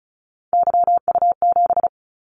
Morse code, heard as a keyed tone, about 36 wpm, 700 Hz. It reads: YU7
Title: Callsign Prefixes for Top 25 DXCC - 35wpm